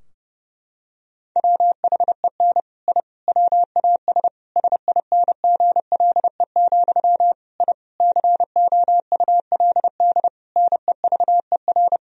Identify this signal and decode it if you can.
Morse code, heard as a keyed tone, about 32 wpm, 700 Hz. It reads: WHEN I WAS SINGLE, I COULD NEVER
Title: Sentences from Top 1000 Words - Review Part 4 - 30wpm